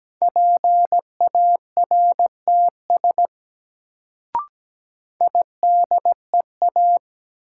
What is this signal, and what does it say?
Morse code, heard as a keyed tone, about 17 wpm, 700 Hz. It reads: PARTS IDEA
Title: Sets of 3 Words - New Word Drill - 17wpm